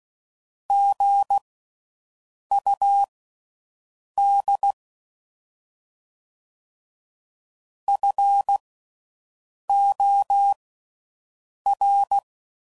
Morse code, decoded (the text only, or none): GUD FOR